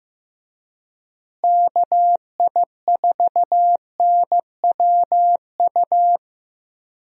Morse code, heard as a keyed tone, about 15 wpm, 700 Hz. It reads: KI4NWU
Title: USA Call Signs - 6 Characters - Repeated 3x - 15wpm